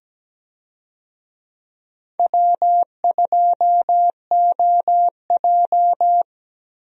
Morse code, heard as a keyed tone, about 17 wpm, 700 Hz. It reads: W2OJ
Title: USA Call Signs - 4 Characters - Repeated 3x - 17wpm